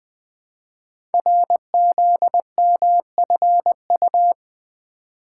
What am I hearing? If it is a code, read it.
Morse code, heard as a keyed tone, about 20 wpm, 700 Hz. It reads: RZMFU
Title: Code Groups - 5 Letter:Numbers - 20wpm